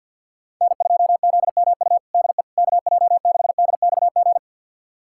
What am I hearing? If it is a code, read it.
Morse code, heard as a keyed tone, about 52 wpm, 700 Hz. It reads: N2ZMU DE KJ6DXG
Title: USA Callsign DE Callsign 50wpm